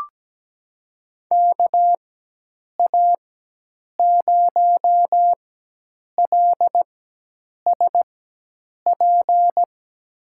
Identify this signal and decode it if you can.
Morse code, heard as a keyed tone, about 17 wpm, spaced as for 8 wpm, 700 Hz. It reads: KA0LSP
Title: USA Call Signs - 6 Characters - 4x Character Spacing - 17wpm